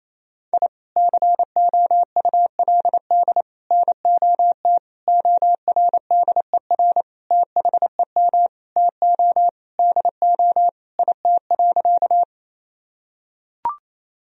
Morse code, heard as a keyed tone, about 28 wpm, 700 Hz. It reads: I COULD NOT ORDER THEM TO DO IT.
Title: Sentences from Top 1000 Words - Review Part 2 - 28wpm